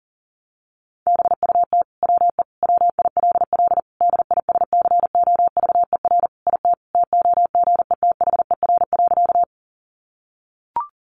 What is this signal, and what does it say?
Morse code, heard as a keyed tone, about 40 wpm, 700 Hz. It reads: BUT WE WILL DISCOVER IT TOGETHER.
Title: Sentences from Top 1100 Words 40wpm